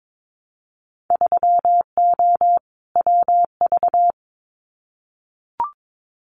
Morse code, heard as a keyed tone, about 22 wpm, 700 Hz. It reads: 3O WV
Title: Winter Field Day Exchanges 22wpm